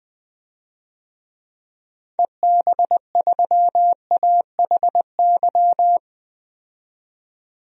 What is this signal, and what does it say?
Morse code, heard as a keyed tone, about 20 wpm, 700 Hz. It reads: EB3AHY
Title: DXCC-100 Call Signs - 6 Characters - 20wpm